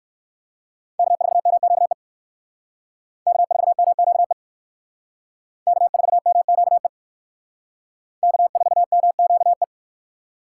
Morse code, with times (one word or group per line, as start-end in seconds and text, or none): 1.0-1.9 K4MQE
3.3-4.3 K4MQE
5.7-6.9 K4MQE
8.2-9.6 K4MQE